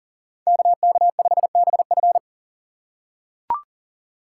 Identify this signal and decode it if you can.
Morse code, heard as a keyed tone, about 40 wpm, 700 Hz. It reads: KK5BF
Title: USA Call Signs - 5 Characters - 6x Character Spacing - 40wpm